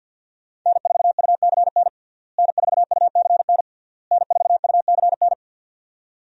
Morse code, heard as a keyed tone, about 52 wpm, 700 Hz. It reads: N4UCN N4UCN N4UCN
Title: USA Call Signs - 5 Characters - Repeated 3x - 50wpm